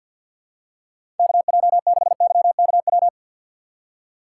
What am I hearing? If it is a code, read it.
Morse code, heard as a keyed tone, about 50 wpm, 700 Hz. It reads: KJ6YKW
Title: USA Call Signs - 6 Characters - 4x Character Spacing - 50wpm